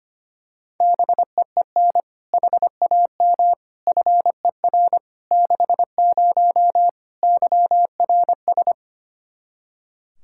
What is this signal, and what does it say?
Morse code, heard as a keyed tone, about 25 wpm, 700 Hz. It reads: BEEN HAM FER 60 YRS
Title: Practice QSO - Exchange 3 - More Info - 25wpm